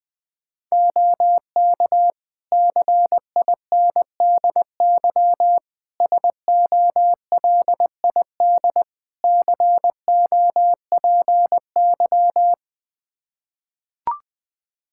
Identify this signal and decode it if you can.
Morse code, heard as a keyed tone, about 20 wpm, 700 Hz. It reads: OK CINDY SOLID COPY
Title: K1USN SST QSO Practice 20wpm